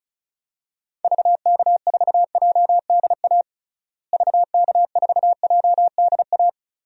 Morse code, heard as a keyed tone, about 36 wpm, 700 Hz. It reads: VK4JDA VK4JDA
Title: DXCC-100 Call Signs - 6 Characters - Repeated 3x - 35wpm